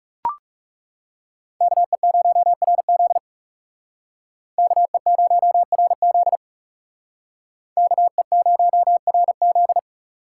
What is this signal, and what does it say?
Morse code, heard as a keyed tone, about 38 wpm, 700 Hz. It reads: KE0RZ KE0RZ KE0RZ